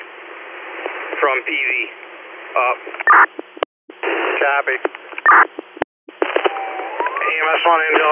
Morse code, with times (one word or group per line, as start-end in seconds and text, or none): none